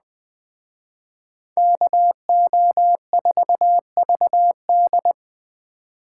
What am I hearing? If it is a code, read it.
Morse code, heard as a keyed tone, about 20 wpm, 700 Hz. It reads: KO4VD